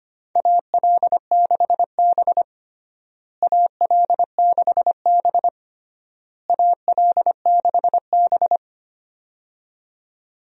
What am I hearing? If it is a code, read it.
Morse code, heard as a keyed tone, about 25 wpm, 700 Hz. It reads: AL6B AL6B AL6B
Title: USA Call Signs - 4 Characters - Repeated 3x - 25wpm